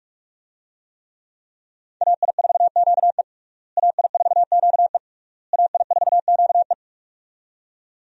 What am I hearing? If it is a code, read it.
Morse code, heard as a keyed tone, about 41 wpm, 700 Hz. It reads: AI4QE AI4QE AI4QE
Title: USA Call Signs - 5 Characters - Repeated 3x - 45wpm